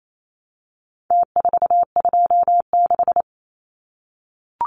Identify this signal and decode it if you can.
Morse code, heard as a keyed tone, about 28 wpm, 700 Hz. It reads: T426
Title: Numbers - Contest Style - 4-Digit 28wpm